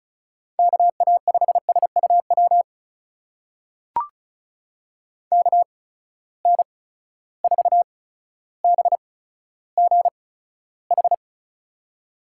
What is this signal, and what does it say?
Morse code, heard as a keyed tone, about 36 wpm, spaced as for 15 wpm, 700 Hz. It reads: KA5SUW KN4BGH
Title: USA Call Signs - 6 Characters - 8x Character Spacing - 35wpm